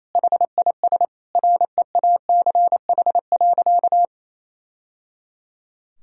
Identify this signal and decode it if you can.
Morse code, heard as a keyed tone, about 28 wpm, 700 Hz. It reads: HIS REACH.
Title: Sentences from Top 1000 Words - Review Part 4 - 28wpm